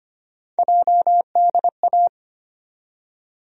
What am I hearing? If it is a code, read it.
Morse code, heard as a keyed tone, about 25 wpm, 700 Hz. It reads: JDA
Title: USA Call Sign Postfixes - 3 Letters 25wpm